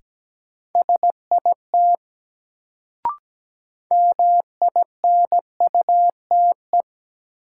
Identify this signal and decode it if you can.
Morse code, heard as a keyed tone, about 17 wpm, 700 Hz. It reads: SIT MINUTE
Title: Sentences from Top 500 Words - New Word Drill - 17wpm